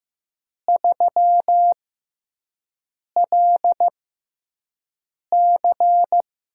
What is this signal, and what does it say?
Morse code, heard as a keyed tone, about 15 wpm, spaced as for 5 wpm, 700 Hz. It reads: 3LC
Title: USA Call Signs - 4 Characters - 6x Character Spacing - 15wpm